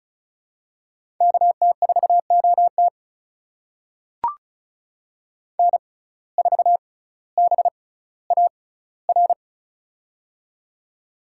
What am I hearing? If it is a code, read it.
Morse code, heard as a keyed tone, about 36 wpm, spaced as for 18 wpm, 700 Hz. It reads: KT4OT N4BAR
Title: USA Call Signs - 5 Characters - 6x Character Spacing - 35wpm